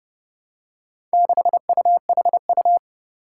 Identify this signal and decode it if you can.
Morse code, heard as a keyed tone, about 31 wpm, 700 Hz. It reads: 6UHU
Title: Four Challenging Letter-Number Combinations 30wpm